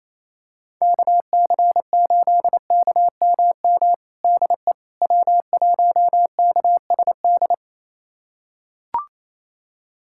Morse code, heard as a keyed tone, about 28 wpm, 700 Hz. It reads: KC8KMM DE W1KSD
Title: USA Callsign DE Callsign 28wpm